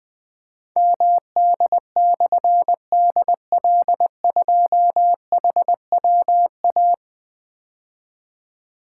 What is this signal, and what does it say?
Morse code, heard as a keyed tone, about 21 wpm, 700 Hz. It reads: MD/DL2HWA
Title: Difficult Call Signs 20wpm